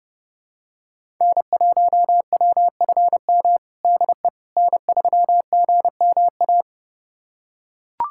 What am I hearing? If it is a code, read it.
Morse code, heard as a keyed tone, about 30 wpm, 700 Hz. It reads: N1WFM DE N3GMA
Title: USA Callsign DE Callsign - 5 Characters - 30wpm